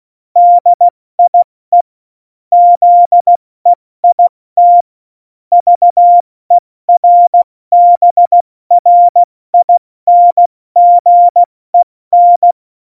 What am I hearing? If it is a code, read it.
Morse code, heard as a keyed tone, about 16 wpm, spaced as for 14 wpm, 700 Hz. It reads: DIE ZEIT VERBRINGEN
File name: Morsekurs / Witze / Witz 26. Januar 2021, 70 F80